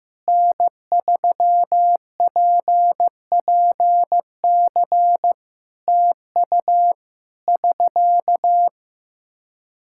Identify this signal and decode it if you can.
Morse code, heard as a keyed tone, about 16 wpm, 700 Hz. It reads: N3PPC TU <SK>
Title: Practice QSO - Exchange 4 - Ending - 15wpm